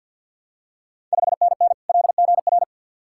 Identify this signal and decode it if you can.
Morse code, heard as a keyed tone, about 51 wpm, 690 Hz. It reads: 5NN LGR
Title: Ontario QSO Party Exchange 50wpm